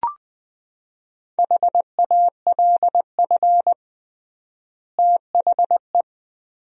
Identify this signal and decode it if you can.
Morse code, heard as a keyed tone, about 20 wpm, 700 Hz. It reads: HALF THE
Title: 3x Word-Spacing - Sentences from Top 500 Words - 20wpm